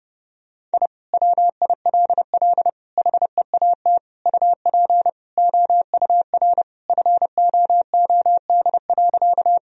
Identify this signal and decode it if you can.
Morse code, heard as a keyed tone, about 29 wpm, 700 Hz. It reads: I WILL HEAT UP OUR FOOD.
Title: Sentences from Top 1000 Words - Review Part 3 - 30wpm